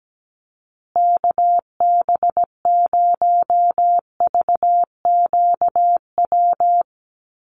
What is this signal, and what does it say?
Morse code, heard as a keyed tone, about 17 wpm, 700 Hz. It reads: KB0VQW
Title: USA Call Signs Repeated 2x 17wpm